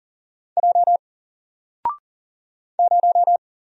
Morse code, heard as a keyed tone, about 40 wpm, 700 Hz. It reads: J 0
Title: Single Letter-Number Alt 40wpm